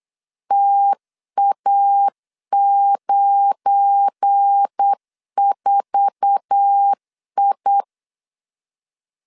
Morse code, heard as a keyed tone, about 8 wpm, 800 Hz. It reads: TA94I
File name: LEZIONI / 8 YJ45 / 8 YJ4 8wpm 3246